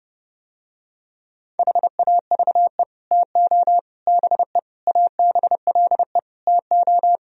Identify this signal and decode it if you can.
Morse code, heard as a keyed tone, about 30 wpm, 700 Hz. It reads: HAVE TO BE ABLE TO